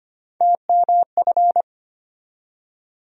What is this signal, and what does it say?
Morse code, heard as a keyed tone, about 25 wpm, 700 Hz. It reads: TMF
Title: USA Call Sign Postfixes - 3 Letters 25wpm